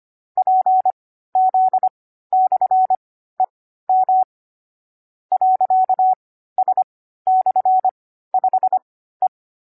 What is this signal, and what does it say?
Morse code, heard as a keyed tone, about 25 wpm, spaced as for 14 wpm, 750 Hz. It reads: PZ/EM .S/5E
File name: CW-Koch / koch.29.3.groups5.15wpm